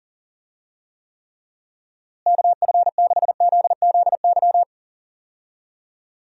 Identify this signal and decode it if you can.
Morse code, heard as a keyed tone, about 40 wpm, 700 Hz. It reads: KF6ZZY
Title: USA Call Signs - 6 Characters - Repeated 3x - 40wpm